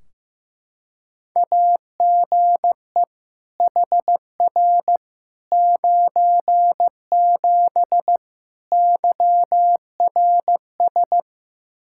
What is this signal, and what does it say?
Morse code, heard as a keyed tone, about 15 wpm, 700 Hz. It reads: AGE HR 97 YRS
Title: Practice QSO - Exchange 3 - More Info - 15wpm